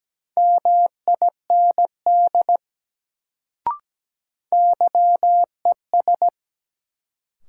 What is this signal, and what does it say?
Morse code, heard as a keyed tone, about 17 wpm, 700 Hz. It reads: MIND YES
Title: Sentences from Top 500 Words - New Word Drill - 17wpm